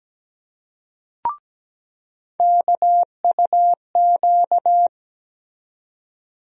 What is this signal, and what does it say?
Morse code, heard as a keyed tone, about 17 wpm, 700 Hz. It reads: KUQ
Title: USA Call Sign Postfixes - 3 Letters 17wpm